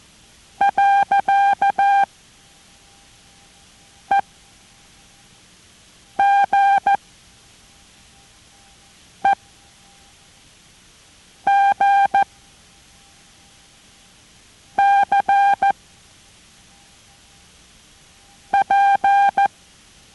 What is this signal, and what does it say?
Morse code, heard as a keyed tone, about 14 wpm, 800 Hz. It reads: . E G E G C P